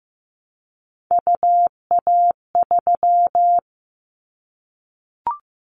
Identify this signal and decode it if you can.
Morse code, heard as a keyed tone, about 15 wpm, 700 Hz. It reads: UA3
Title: Callsign Prefixes for Top 50 DXCC - 15wpm